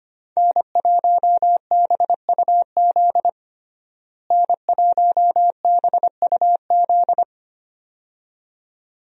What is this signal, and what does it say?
Morse code, heard as a keyed tone, about 25 wpm, 690 Hz. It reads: N1BUZ N1BUZ
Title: USA Call Signs - 5 Characters - Repeated 2x - 25wpm